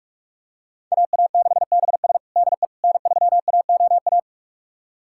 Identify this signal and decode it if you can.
Morse code, heard as a keyed tone, about 46 wpm, 700 Hz. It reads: AA6BS DE N3AOA
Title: USA Callsign DE Callsign - 5 Characters - 45wpm